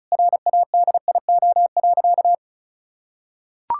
Morse code, heard as a keyed tone, about 36 wpm, 700 Hz. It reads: RADIO.
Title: Sentences from Top 1000 Words - Review Part 5 - 35wpm